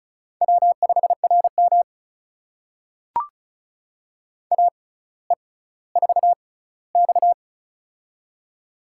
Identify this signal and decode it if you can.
Morse code, heard as a keyed tone, about 36 wpm, spaced as for 18 wpm, 700 Hz. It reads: W5RM AE4X
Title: USA Call Signs - 4 Characters - 6x Character Spacing - 35wpm